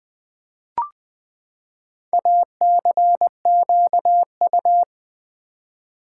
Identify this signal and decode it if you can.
Morse code, heard as a keyed tone, about 20 wpm, 700 Hz. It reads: ACQU
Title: Build A Word - 4 Letters At A Time - 11-Letter Words - 20wpm